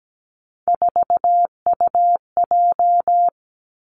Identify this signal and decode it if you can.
Morse code, heard as a keyed tone, about 17 wpm, 700 Hz. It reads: 4UJ
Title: Three Challenging Letter-Number Combinations 17wpm